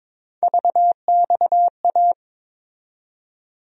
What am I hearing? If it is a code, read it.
Morse code, heard as a keyed tone, about 22 wpm, 700 Hz. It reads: VXA
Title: USA Call Sign Postfixes - 3 Letters 22wpm